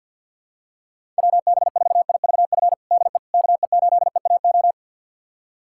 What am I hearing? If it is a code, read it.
Morse code, heard as a keyed tone, about 46 wpm, 700 Hz. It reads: WB4IVF DE KE8EAO
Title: USA Callsign DE Callsign 50wpm